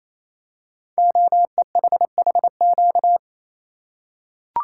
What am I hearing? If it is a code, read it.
Morse code, heard as a keyed tone, about 28 wpm, 700 Hz. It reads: OEHHQ
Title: Code Groups - 5 Letters - 28wpm